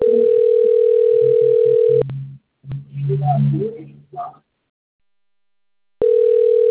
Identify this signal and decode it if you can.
Ringback tone (North America), 440 + 480 Hz.